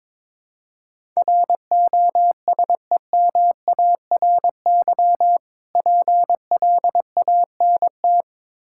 Morse code, heard as a keyed tone, about 22 wpm, 690 Hz. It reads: ROSEMARY PLANT
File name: Difficult Words with Hints - 22wpm